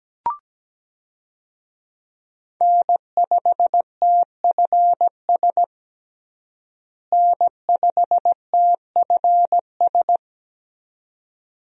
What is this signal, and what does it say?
Morse code, heard as a keyed tone, about 17 wpm, 700 Hz. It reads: N5TFS N5TFS
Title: USA Call Signs - 5 Characters - Repeated 2x - 17wpm